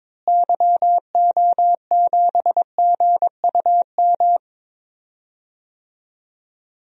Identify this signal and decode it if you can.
Morse code, heard as a keyed tone, about 22 wpm, 700 Hz. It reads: YO7GUM
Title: DXCC-100 Call Signs - 6 Characters - Repeated 2x - 22wpm